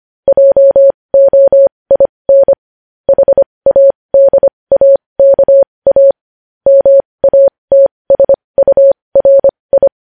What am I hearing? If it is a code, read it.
Morse code, heard as a keyed tone, about 25 wpm, spaced as for 20 wpm, 550 Hz. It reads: JOIN HADAKA MATSURI